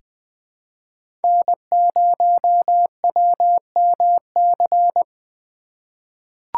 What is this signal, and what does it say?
Morse code, heard as a keyed tone, about 20 wpm, 700 Hz. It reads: N0WMC
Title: USA Call Signs - 5 Characters - 2x Character Spacing - 20wpm